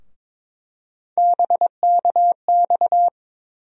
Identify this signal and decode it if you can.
Morse code, heard as a keyed tone, about 22 wpm, 700 Hz. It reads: BKX
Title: USA Call Sign Postfixes - 3 Letters 22wpm